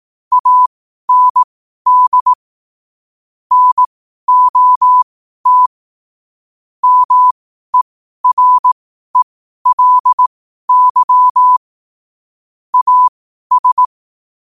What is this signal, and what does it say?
Morse code, heard as a keyed tone, about 18 wpm, spaced as for 12 wpm, 1000 Hz. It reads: AND NOT MERELY AS
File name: qotdfiles / 12wpm